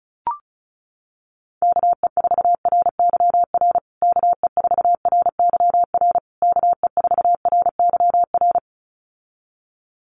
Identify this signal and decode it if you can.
Morse code, heard as a keyed tone, about 34 wpm, 700 Hz. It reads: KE4RYR KE4RYR KE4RYR
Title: USA Call Signs Repeated 3x 35wpm